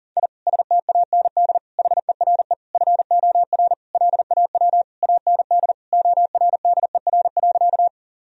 Morse code, heard as a keyed tone, about 40 wpm, 700 Hz. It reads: I STAND HERE FOR LAW AND ORDER.
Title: Sentences from Top 1000 Words - Review Part 3 - 40wpm